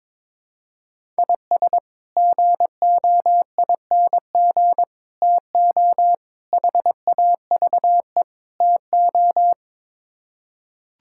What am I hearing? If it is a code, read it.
Morse code, heard as a keyed tone, about 22 wpm, 700 Hz. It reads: IS GOING TO HAVE TO
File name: practice / 22wpm / Build A Phrase - 5-Word Phrases - 22wpm